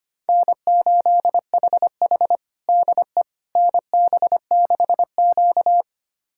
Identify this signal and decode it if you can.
Morse code, heard as a keyed tone, about 25 wpm, 700 Hz. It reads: N8HH DE NB6Q
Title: USA Callsign DE Callsign - 4 Characters - 25wpm